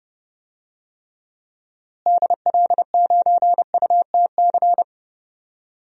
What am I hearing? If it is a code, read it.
Morse code, heard as a keyed tone, about 32 wpm, 700 Hz. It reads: DL9UTC
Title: DXCC-100 Call Signs - 6 Characters - Repeated 3x - 30wpm